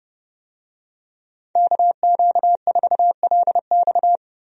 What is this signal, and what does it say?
Morse code, heard as a keyed tone, about 29 wpm, 700 Hz. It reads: KQ4LX
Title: USA Call Signs - 5 Characters - 30wpm